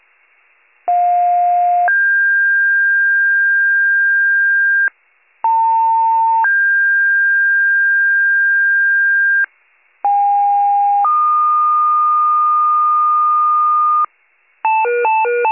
Two-tone page, heard as a one-second tone, then a three-second tone, about 710 then 1600 Hz, about 900 then 1600 Hz, about 810 then 1220 Hz, with noise. It is heard from 0.9 to 14.1 s.